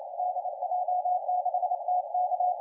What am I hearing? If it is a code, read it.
Morse code, heard as a keyed tone, about 28 wpm, 720 Hz, with radio noise. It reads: DJ6TK
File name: la4c / sac-cw-2017 / CONTEST AUDIO 1100Z